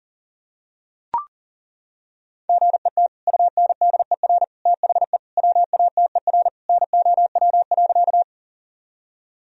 Morse code, heard as a keyed tone, about 40 wpm, 700 Hz. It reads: GET UNDER THE WATER NOW.
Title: Sentences from Top 1000 Words - Review Part 1 - 40wpm